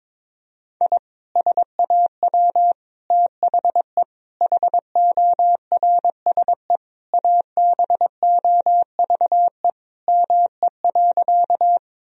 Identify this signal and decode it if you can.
Morse code, heard as a keyed tone, about 22 wpm, 700 Hz. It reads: I SAW THE HORSE ABOVE ME.
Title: Sentences from Top 1000 Words - Review Part 2 - 22wpm